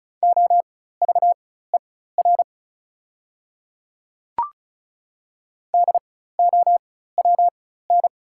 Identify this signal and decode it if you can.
Morse code, heard as a keyed tone, about 36 wpm, spaced as for 16 wpm, 690 Hz. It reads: OVER DOWN
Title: 4x Character Spacing - Top 300 Words - 35wpm